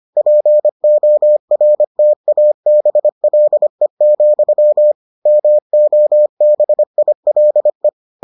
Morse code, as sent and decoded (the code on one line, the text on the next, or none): .--. --- .-. - .- -... .-.. --..-- / -- --- -... .. .-.. .
PORTABLE, MOBILE